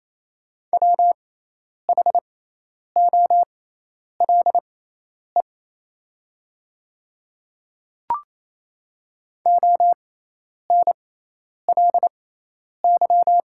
Morse code, heard as a keyed tone, about 28 wpm, spaced as for 10 wpm, 700 Hz. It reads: WHOLE ONLY
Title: 6x Character Spacing - Top 500 Words - 28wpm